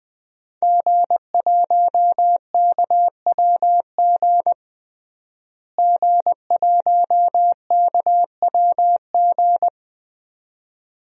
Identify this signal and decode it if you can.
Morse code, heard as a keyed tone, about 20 wpm, 700 Hz. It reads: G1KWG G1KWG
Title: DXCC-100 Call Signs - 5 Characters - Repeated 2x - 20wpm